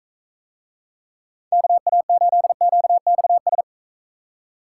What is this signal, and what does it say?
Morse code, heard as a keyed tone, about 45 wpm, 700 Hz. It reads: KA8QXS